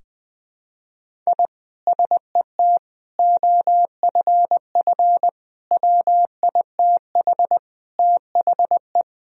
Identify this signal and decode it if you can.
Morse code, heard as a keyed tone, about 21 wpm, 700 Hz. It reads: I SET OFF WITH THE
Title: Sentences from Top 1000 Words - Review Part 1 - 20wpm